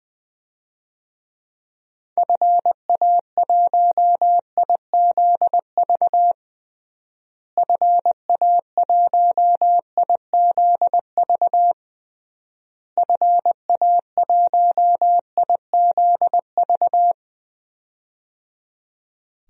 Morse code, heard as a keyed tone, about 20 wpm, 700 Hz. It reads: FA1IZV FA1IZV FA1IZV
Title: DXCC-100 Call Signs - 6 Characters - Repeated 3x - 20wpm